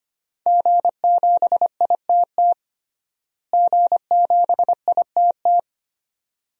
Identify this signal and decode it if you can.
Morse code, heard as a keyed tone, about 25 wpm, 700 Hz. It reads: G7ITT G7ITT
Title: DXCC-100 Call Signs - 5 Characters - Repeated 2x - 25wpm